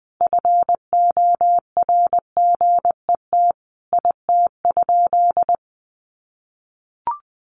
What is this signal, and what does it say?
Morse code, heard as a keyed tone, about 20 wpm, 700 Hz. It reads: FORGET IT?
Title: Sentences from Top 1200 Words 20wpm